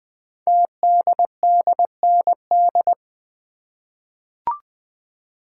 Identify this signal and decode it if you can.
Morse code, heard as a keyed tone, about 20 wpm, 700 Hz. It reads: TDDND
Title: Code Groups - 5 Letters - 20wpm